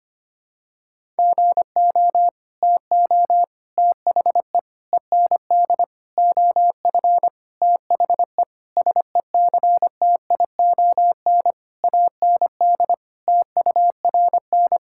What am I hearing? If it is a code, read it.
Morse code, heard as a keyed tone, about 25 wpm, 700 Hz. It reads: GO TO THE END OF THE SECTION AND TURN
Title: Sentences from Top 1000 Words - Review Part 5 - 25wpm